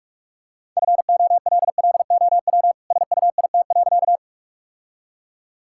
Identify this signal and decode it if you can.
Morse code, heard as a keyed tone, about 41 wpm, 700 Hz. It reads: FOLLOW SUIT.
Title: Sentences from Top 1600 Words 45wpm